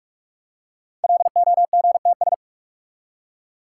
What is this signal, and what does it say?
Morse code, heard as a keyed tone, about 44 wpm, 700 Hz. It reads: LOGTS
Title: Code Groups - 5 Letter:Numbers - 45wpm